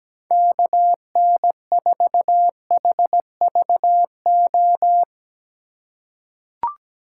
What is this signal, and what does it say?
Morse code, heard as a keyed tone, about 17 wpm, 700 Hz. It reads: KN4HVO
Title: USA Call Signs - 6 Characters - 2x Character Spacing - 17wpm